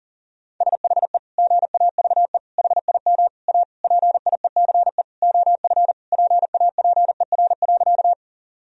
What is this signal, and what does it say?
Morse code, heard as a keyed tone, about 40 wpm, 700 Hz. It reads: SHE GAVE HIM A PIECE OF PAPER.
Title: Sentences from Top 1000 Words - Review Part 2 - 40wpm